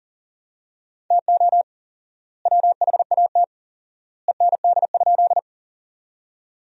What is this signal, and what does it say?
Morse code, heard as a keyed tone, about 40 wpm, 700 Hz. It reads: TO WHAT END?
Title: 4x Word-Spacing - Sentences from Top 200 Words - 40wpm